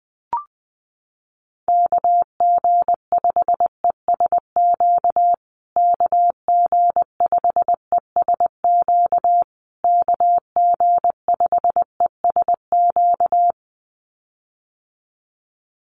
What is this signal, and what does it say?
Morse code, heard as a keyed tone, about 20 wpm, 700 Hz. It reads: KG5ESQ KG5ESQ KG5ESQ
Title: USA Call Signs Repeated 3x 20wpm